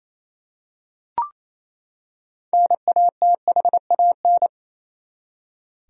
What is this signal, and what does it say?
Morse code, heard as a keyed tone, about 28 wpm, 700 Hz. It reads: NATHAN
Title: USA Common Names - 28wpm